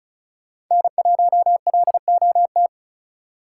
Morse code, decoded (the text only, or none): N1LOT